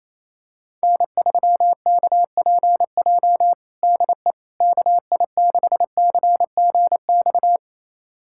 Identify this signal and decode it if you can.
Morse code, heard as a keyed tone, about 27 wpm, 700 Hz. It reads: N3KPJ DE KI6CGX
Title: USA Callsign DE Callsign 28wpm